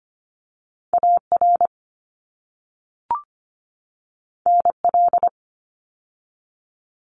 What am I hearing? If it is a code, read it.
Morse code, heard as a keyed tone, about 25 wpm, 700 Hz. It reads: AR NL
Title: ARRL Sections 25wpm